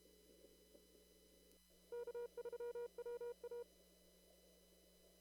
Morse code, heard as a keyed tone, about 33 wpm, 480 Hz. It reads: K3WA
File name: CW / kt5j wpx cw-op-k5pi